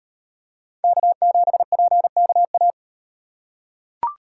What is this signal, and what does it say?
Morse code, heard as a keyed tone, about 38 wpm, 700 Hz. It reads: K7PKA